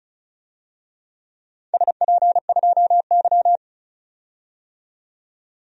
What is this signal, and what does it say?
Morse code, heard as a keyed tone, about 34 wpm, 700 Hz. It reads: SP2Y
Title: DXCC-100 Call Signs - 4 Characters - 35wpm